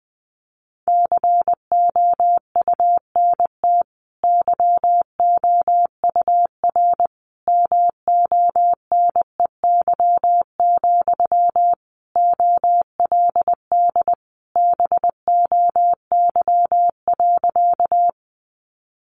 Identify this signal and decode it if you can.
Morse code, heard as a keyed tone, about 20 wpm, 700 Hz. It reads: COUNT YOUR MONEY, OLD BOY.
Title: Sentences from Top 1500 Words 20wpm